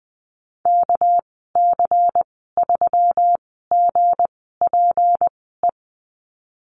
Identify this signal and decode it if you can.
Morse code, heard as a keyed tone, about 20 wpm, spaced as for 15 wpm, 700 Hz. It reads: KC3GPE
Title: 207 Lesson 33g - Letter j - Call Signs 20wpm